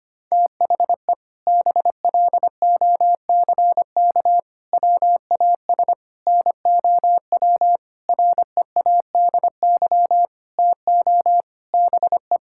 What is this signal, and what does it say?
Morse code, heard as a keyed tone, about 25 wpm, 700 Hz. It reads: THE BLOCK WAS NOW READY TO BE
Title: Sentences from Top 1600 Words 25wpm